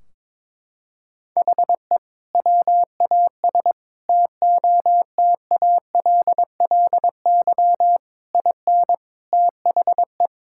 Morse code, heard as a keyed tone, about 22 wpm, 700 Hz. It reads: HE WAS TOTALLY IN THE
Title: Sentences from Top 1600 Words 22wpm